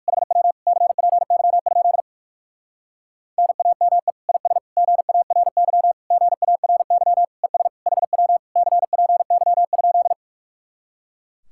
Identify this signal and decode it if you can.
Morse code, heard as a keyed tone, about 58 wpm, 700 Hz. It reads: HW CPY? NAME IS GARY GARY ES HW CPY?